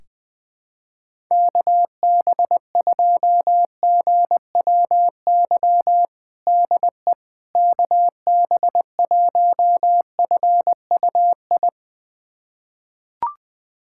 Morse code, heard as a keyed tone, about 21 wpm, 700 Hz. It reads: KB2GWY DE KB1FUI
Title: USA Callsign DE Callsign - 6 Characters - 20wpm